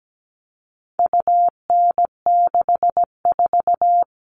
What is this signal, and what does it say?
Morse code, heard as a keyed tone, about 17 wpm, 700 Hz. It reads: UN64
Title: All Cut Numbers - 4-Digit 17wpm